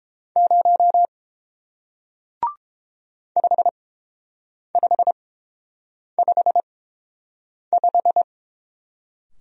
Morse code, heard as a keyed tone, about 31 wpm, spaced as for 7 wpm, 700 Hz. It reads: 0 5555